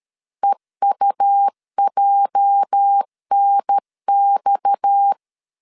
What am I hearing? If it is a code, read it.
Morse code, heard as a keyed tone, about 13 wpm, 800 Hz. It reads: EUJNX
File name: CW / DET / 1 9 1500C S A 12wpm 3800